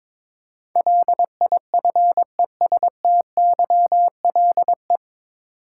Morse code, heard as a keyed tone, about 22 wpm, 700 Hz. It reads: LIFESTYLE
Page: Morse Code Ninja practice 22wpm